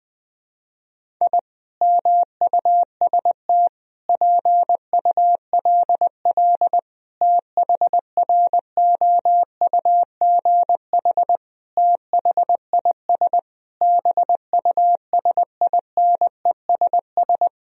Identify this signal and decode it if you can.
Morse code, heard as a keyed tone, about 20 wpm, 700 Hz. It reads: I MUST PULL THROUGH THIS BUSINESS
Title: Sentences from Top 1000 Words - Review Part 2 - 20wpm